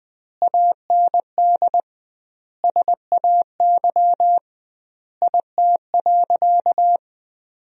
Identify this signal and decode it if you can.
Morse code, heard as a keyed tone, about 20 wpm, 700 Hz. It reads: AND SAY IT.
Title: 2x Word-Spacing - Sentences from Top 400 Words - 20wpm